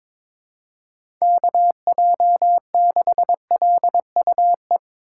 Morse code, heard as a keyed tone, about 22 wpm, 700 Hz. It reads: KJ6LUE
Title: USA Call Signs - 6 Characters - 4x Character Spacing - 22wpm